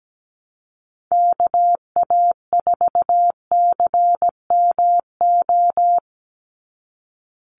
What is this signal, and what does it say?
Morse code, heard as a keyed tone, about 17 wpm, 700 Hz. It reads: KA4CMO
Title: USA Call Signs 17wpm